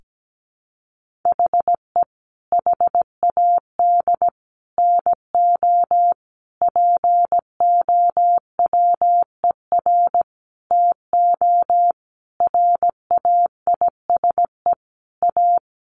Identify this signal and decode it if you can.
Morse code, heard as a keyed tone, about 17 wpm, 700 Hz. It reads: HE HAD NO POWER TO RAISE A
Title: Sentences from Top 1400 Words 17wpm